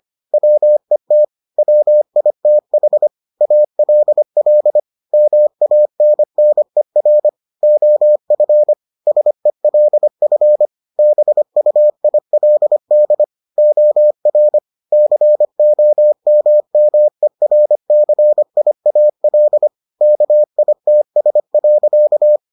Morse code, heard as a keyed tone, about 25 wpm, 600 Hz. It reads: WET WITH ALL MANNER OF SELF BUILD OR COMMERCIAL KITS.